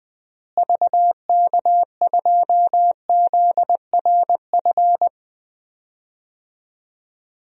Morse code, heard as a keyed tone, about 20 wpm, 700 Hz. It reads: VK2ZRF
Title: DXCC-100 Call Signs - 6 Characters - Repeated 2x - 20wpm